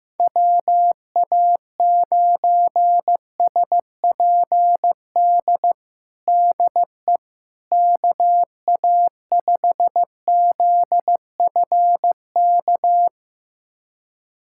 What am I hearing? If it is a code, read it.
Morse code, heard as a keyed tone, about 16 wpm, 700 Hz. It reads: WA9SPD DE KA5ZFK
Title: USA Callsign DE Callsign - 6 Characters - 15wpm